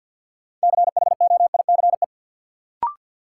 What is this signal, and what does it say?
Morse code, heard as a keyed tone, about 48 wpm, 700 Hz. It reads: CHOICE
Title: Sentences from Top 900 Words - New Word Drill - 50wpm